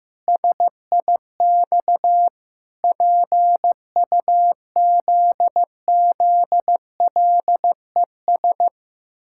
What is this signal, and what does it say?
Morse code, heard as a keyed tone, about 15 wpm, 700 Hz. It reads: SIX PUZZLES
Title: Pangram Challenge v2 - 15wpm